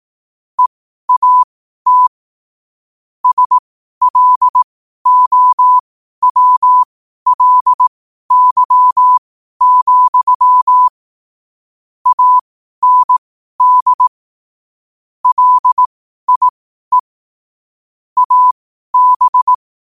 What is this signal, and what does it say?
Morse code, heard as a keyed tone, about 18 wpm, spaced as for 12 wpm, 1000 Hz. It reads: EAT SLOWLY, AND LIE AB
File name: qotdfiles / 12wpm